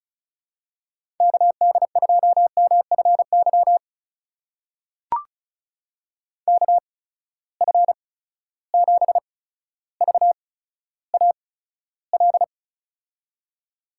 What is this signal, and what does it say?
Morse code, heard as a keyed tone, about 36 wpm, spaced as for 15 wpm, 700 Hz. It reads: KD2MFY KF7VAL
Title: USA Call Signs - 6 Characters - 8x Character Spacing - 35wpm